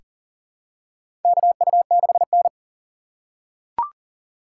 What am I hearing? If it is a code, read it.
Morse code, heard as a keyed tone, about 40 wpm, 700 Hz. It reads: KU6N
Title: USA Call Signs - 4 Characters - 4x Character Spacing - 40wpm